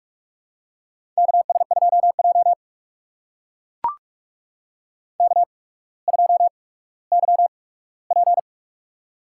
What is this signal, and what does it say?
Morse code, heard as a keyed tone, about 44 wpm, spaced as for 19 wpm, 700 Hz. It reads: KS2J K2YP
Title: USA Call Signs - 4 Characters - 8x Character Spacing - 45wpm